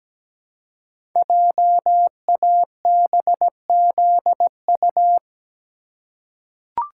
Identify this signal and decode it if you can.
Morse code, heard as a keyed tone, about 17 wpm, 700 Hz. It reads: JABZU
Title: Code Groups - 5 Letters - 17wpm